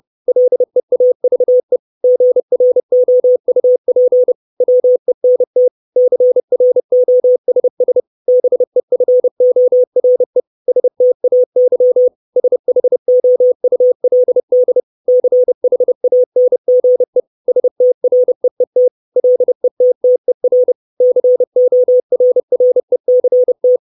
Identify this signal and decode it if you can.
Morse code, heard as a keyed tone, about 30 wpm, 500 Hz. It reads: LEAVE GROUP WENT CROSS BEFORE STAY SHOULD CHANGE STREET LETTER CORRECT
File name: 632 Words 30WPM